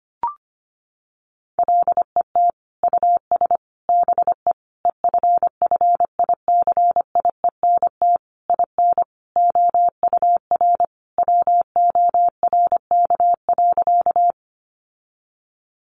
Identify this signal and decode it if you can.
Morse code, heard as a keyed tone, about 25 wpm, 700 Hz. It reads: LET US BE EFFICIENT IN OUR WORK.
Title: Sentences from Top 1300 Words 25wpm